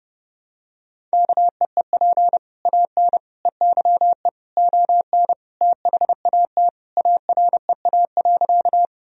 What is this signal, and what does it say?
Morse code, heard as a keyed tone, about 30 wpm, 700 Hz. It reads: KEEP AN EYE ON THAT AREA.
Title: Sentences from Top 1000 Words - Review Part 2 - 30wpm